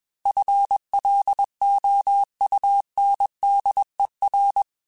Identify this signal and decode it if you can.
Morse code, heard as a keyed tone, about 21 wpm, 800 Hz. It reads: FLOUNDER
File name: FL Runs / FL Runs mp3s / FL Runs @20